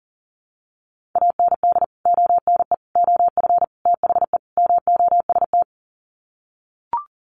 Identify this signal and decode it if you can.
Morse code, heard as a keyed tone, about 40 wpm, 700 Hz. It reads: AND ONE OF THE MOST
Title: Sets of 5 Words Encore 40wpm